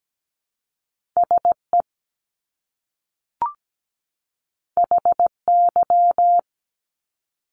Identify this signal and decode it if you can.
Morse code, heard as a keyed tone, about 17 wpm, 700 Hz. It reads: SE HY
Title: Top Two Letter Combinations in Words 17wpm